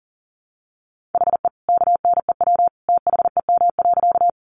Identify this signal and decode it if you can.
Morse code, heard as a keyed tone, about 39 wpm, 700 Hz. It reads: HE KNEW THEM.
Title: Sentences from Top 400 Words 40wpm